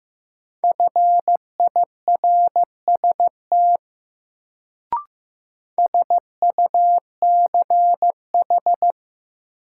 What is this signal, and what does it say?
Morse code, heard as a keyed tone, about 15 wpm, 700 Hz. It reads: FIRST SUCH
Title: Sets of 2 Words - New Word Drill - 15wpm